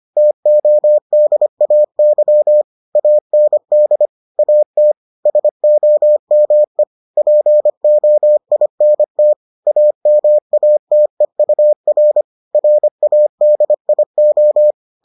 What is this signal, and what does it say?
Morse code, heard as a keyed tone, about 25 wpm, 600 Hz. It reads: TODAY AND AT SOME POINT AMATEUR RADIO